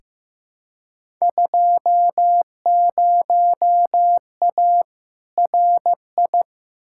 Morse code, heard as a keyed tone, about 15 wpm, 700 Hz. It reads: 20A RI
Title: Field Day Exchanges Encore - 15wpm